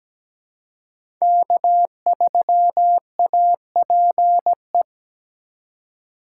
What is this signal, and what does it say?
Morse code, heard as a keyed tone, about 17 wpm, 700 Hz. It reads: K3APE
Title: USA Call Signs - 5 Characters - 4x Character Spacing - 17wpm